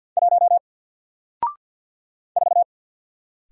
Morse code, heard as a keyed tone, about 48 wpm, 700 Hz. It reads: 1 4
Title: Single Letter-Number Alt 50wpm